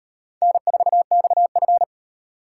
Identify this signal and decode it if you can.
Morse code, heard as a keyed tone, about 34 wpm, 700 Hz. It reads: N4XF